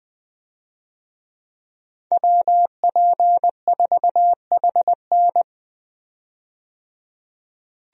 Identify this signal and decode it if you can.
Morse code, heard as a keyed tone, about 20 wpm, 700 Hz. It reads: WP4HN
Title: USA Call Signs - 5 Characters - 20wpm